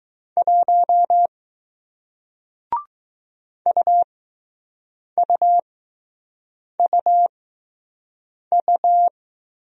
Morse code, heard as a keyed tone, about 20 wpm, spaced as for 6 wpm, 700 Hz. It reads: 1 UUUU